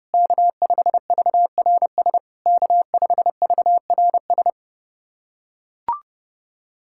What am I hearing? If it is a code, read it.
Morse code, heard as a keyed tone, about 30 wpm, 690 Hz. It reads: K5VRS K5VRS
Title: SOTA QSO Practice 30wpm